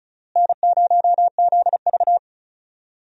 Morse code, heard as a keyed tone, about 34 wpm, 700 Hz. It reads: N0ZV
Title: USA Call Signs - 4 Characters - Repeated 2x - 35wpm